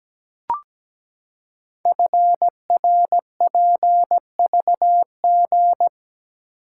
Morse code, heard as a keyed tone, about 17 wpm, 700 Hz. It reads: FRPVG
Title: Code Groups - 5 Letters - 17wpm